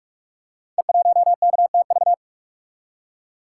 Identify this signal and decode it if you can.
Morse code, heard as a keyed tone, about 42 wpm, 700 Hz. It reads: E1KTV
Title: Code Groups - 5 Letter:Numbers - 45wpm